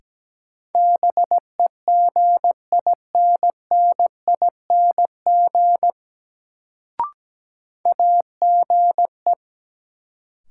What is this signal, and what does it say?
Morse code, heard as a keyed tone, about 17 wpm, 700 Hz. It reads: BEGINNING AGE
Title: Sets of 3 Words - New Word Drill - 17wpm